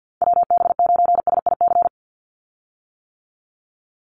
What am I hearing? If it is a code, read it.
Morse code, heard as a keyed tone, about 49 wpm, 700 Hz. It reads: WB9SIC
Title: USA Call Signs 50wpm